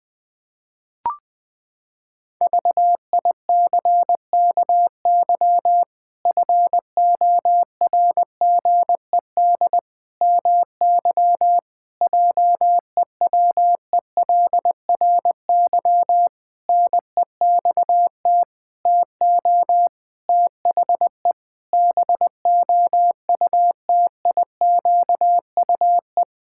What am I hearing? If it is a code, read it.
Morse code, heard as a keyed tone, about 21 wpm, 700 Hz. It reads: VICKY FORGED MY JEWELRY NEXT TO THE BOUTIQUE
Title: Pangram Challenge v2 - 20wpm